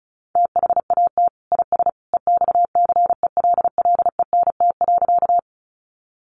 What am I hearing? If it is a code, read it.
Morse code, heard as a keyed tone, about 34 wpm, 700 Hz. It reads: THAT IS EXCELLENT.
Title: Sentences from Top 1000 Words 35wpm